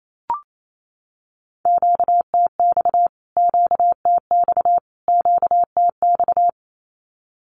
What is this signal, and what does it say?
Morse code, heard as a keyed tone, about 28 wpm, 700 Hz. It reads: QTX QTX QTX
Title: Q-Codes Repeated 3x 28wpm